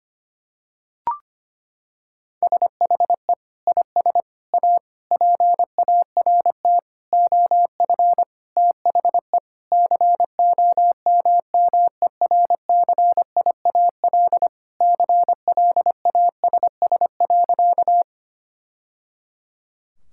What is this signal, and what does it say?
Morse code, heard as a keyed tone, about 25 wpm, 700 Hz. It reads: SHE IS A PART OF THE COMMERCIAL CLASS.
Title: Sentences from Top 1000 Words - Review Part 5 - 25wpm